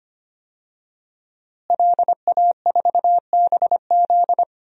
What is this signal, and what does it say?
Morse code, heard as a keyed tone, about 25 wpm, 700 Hz. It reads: LA4BZ
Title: DXCC-100 Call Signs - 5 Characters - Repeated 2x - 25wpm